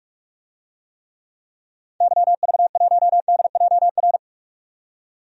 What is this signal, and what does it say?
Morse code, heard as a keyed tone, about 41 wpm, 700 Hz. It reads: YV1DJR
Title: DXCC-100 Call Signs - 6 Characters - 45wpm